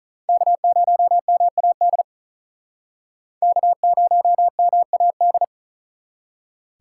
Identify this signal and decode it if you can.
Morse code, heard as a keyed tone, about 45 wpm, 700 Hz. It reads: K0MAD K0MAD